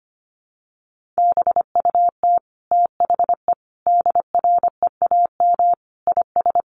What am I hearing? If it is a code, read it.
Morse code, heard as a keyed tone, about 25 wpm, 700 Hz. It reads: BUT THE DREAM IS
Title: Sentences from Top 1500 Words 25wpm